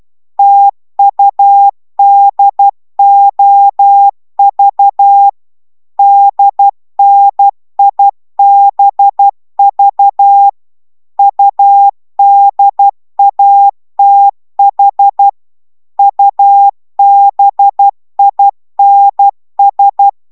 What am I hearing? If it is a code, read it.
Morse code, heard as a keyed tone, about 12 wpm, 800 Hz. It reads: TUDOV DNIBV UDATH UBINS